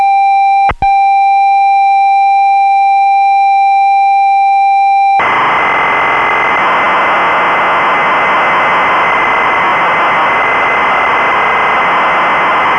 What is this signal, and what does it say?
Computer data on cassette.